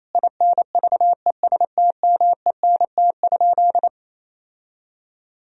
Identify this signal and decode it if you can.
Morse code, heard as a keyed tone, about 28 wpm, 700 Hz. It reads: INVESTMENT?
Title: Sentences from Top 1000 Words - Review Part 4 - 28wpm